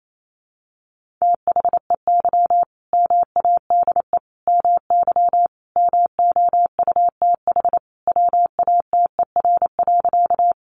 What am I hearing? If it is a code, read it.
Morse code, heard as a keyed tone, about 28 wpm, 700 Hz. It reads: THEY MADE MY MOUTH WATER.
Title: Sentences from Top 1500 Words 28wpm